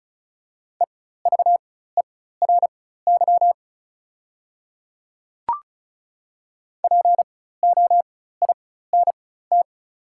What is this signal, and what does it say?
Morse code, heard as a keyed tone, about 36 wpm, spaced as for 16 wpm, 700 Hz. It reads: EVERY POINT
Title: 4x Character Spacing - Top 500 Words - 35wpm